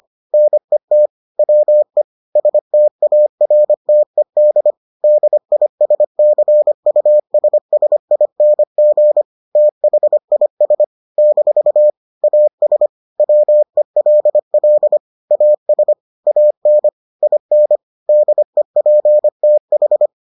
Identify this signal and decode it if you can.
Morse code, heard as a keyed tone, about 25 wpm, 600 Hz. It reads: NET WE STARTED DISCUSSING THIS - AS WELL AS AN IN DEPTH